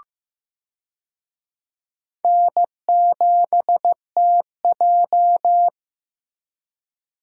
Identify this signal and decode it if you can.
Morse code, heard as a keyed tone, about 15 wpm, 700 Hz. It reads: N7TJ
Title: USA Call Signs - 4 Characters - 15wpm